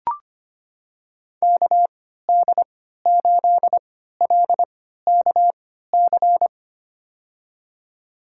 Morse code, heard as a keyed tone, about 25 wpm, spaced as for 14 wpm, 700 Hz. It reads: KD8LKC
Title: USA Call Signs - 6 Characters - 3x Character Spacing - 25wpm